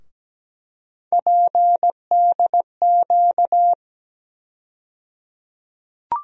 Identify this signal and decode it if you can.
Morse code, heard as a keyed tone, about 17 wpm, 700 Hz. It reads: PDQ
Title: USA Call Sign Postfixes - 3 Letters 17wpm